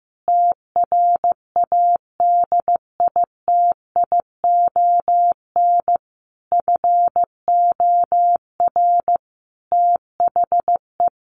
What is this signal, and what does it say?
Morse code, heard as a keyed tone, about 15 wpm, 700 Hz. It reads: TRADITION FOR THE
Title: Sentences from Top 1500 Words 15wpm